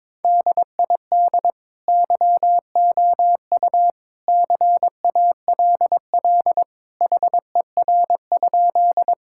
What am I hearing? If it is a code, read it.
Morse code, heard as a keyed tone, about 21 wpm, 700 Hz. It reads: DID YOU CALL HER?
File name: practice / 22wpm / Sentences with 2-Word Phrases - 22wpm